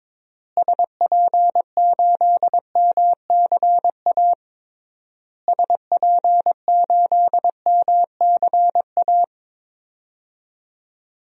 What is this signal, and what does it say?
Morse code, heard as a keyed tone, about 22 wpm, 700 Hz. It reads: SP8MCA SP8MCA
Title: DXCC-100 Call Signs - 6 Characters - Repeated 3x - 22wpm